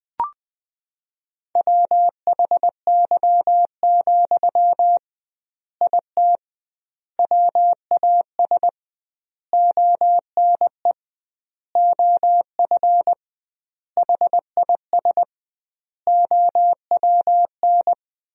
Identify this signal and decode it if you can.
Morse code, heard as a keyed tone, about 20 wpm, 700 Hz. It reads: WHY, IT WAS ONE OF HIS OWN
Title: 2x Word-Spacing - Sentences from Top 300 Words - 20wpm